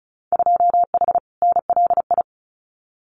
Morse code, heard as a keyed tone, about 36 wpm, 700 Hz. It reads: 2H NLI
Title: Winter Field Day Exchanges Encore 35wpm